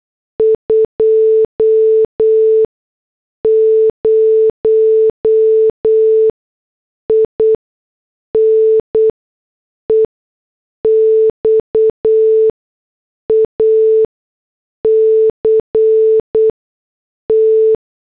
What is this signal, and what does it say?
Morse code, heard as a keyed tone, about 8 wpm, spaced as for 6 wpm, 440 Hz. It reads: 20 IN EXACT
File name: morse / headlines